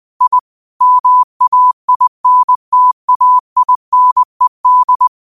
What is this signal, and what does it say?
Morse code, heard as a keyed tone, about 20 wpm, 1000 Hz. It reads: I MAINTAINED
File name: qotdfiles / 20wpm